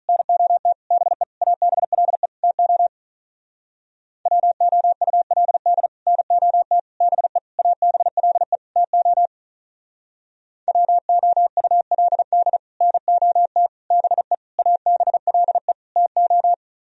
Morse code, heard as a keyed tone, about 46 wpm, 700 Hz. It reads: NOT BE ABLE TO WOULD NOT BE ABLE TO WOULD NOT BE ABLE TO